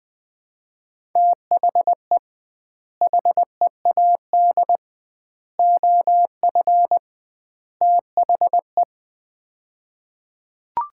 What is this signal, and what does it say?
Morse code, heard as a keyed tone, about 20 wpm, 700 Hz. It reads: THE HEAD OF THE
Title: 2x Word-Spacing - Sets of 4 Words - 20wpm